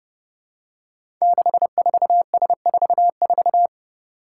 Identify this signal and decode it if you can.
Morse code, heard as a keyed tone, about 29 wpm, 700 Hz. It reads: 64S44
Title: Five Challenging Letter-Number Combinations 30wpm